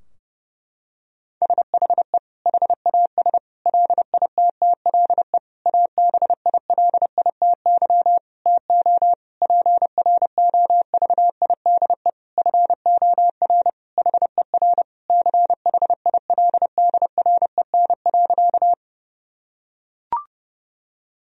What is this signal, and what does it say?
Morse code, heard as a keyed tone, about 31 wpm, 700 Hz. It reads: SHE HAS LITTLE ABILITY TO PROVIDE FOR HER CHILDREN.
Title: Sentences from Top 1000 Words - Review Part 3 - 30wpm